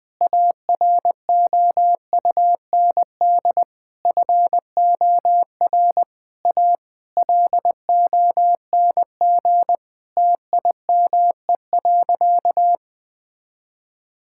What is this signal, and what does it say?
Morse code, heard as a keyed tone, about 21 wpm, 700 Hz. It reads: AROUND FOR A LONG TIME.
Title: Sentences with 7-Word Phrases - 20wpm